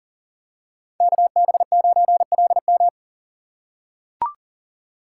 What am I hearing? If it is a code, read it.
Morse code, heard as a keyed tone, about 40 wpm, 700 Hz. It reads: KB9LM
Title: USA Call Signs - 2x Character Spacing - 40wpm